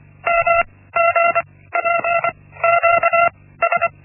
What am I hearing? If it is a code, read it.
Morse code, heard as a keyed tone, about 24 wpm, spaced as for 16 wpm, 680 Hz. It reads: MGPQS
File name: Linksmakalnis-1